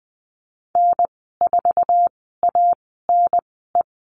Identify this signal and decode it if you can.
Morse code, heard as a keyed tone, about 20 wpm, spaced as for 15 wpm, 700 Hz. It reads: N4ANE
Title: 055 Lesson 10g - Letter r - Call Signs 20wpm